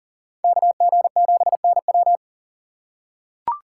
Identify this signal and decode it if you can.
Morse code, heard as a keyed tone, about 39 wpm, 700 Hz. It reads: KG7NW
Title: USA Call Signs - 5 Characters - 6x Character Spacing - 40wpm